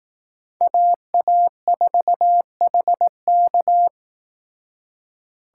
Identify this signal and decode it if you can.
Morse code, heard as a keyed tone, about 18 wpm, 700 Hz. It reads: AA4HK
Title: Speed Racing - USA Call Signs 15wpm